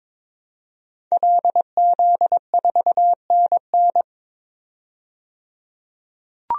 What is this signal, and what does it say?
Morse code, heard as a keyed tone, about 22 wpm, 700 Hz. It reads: LZ4NN
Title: DXCC-100 Call Signs - 5 Characters - 22wpm